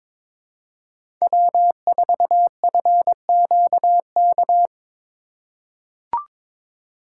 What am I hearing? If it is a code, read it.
Morse code, heard as a keyed tone, about 22 wpm, 700 Hz. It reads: W4FQK
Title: USA Call Signs - 5 Characters - 4x Character Spacing - 22wpm